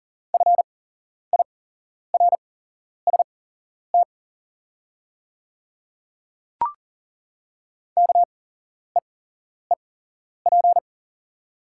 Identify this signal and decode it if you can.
Morse code, heard as a keyed tone, about 38 wpm, spaced as for 11 wpm, 700 Hz. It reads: FIRST KEEP
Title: 8x Character Spacing - Top 300 Words - 40wpm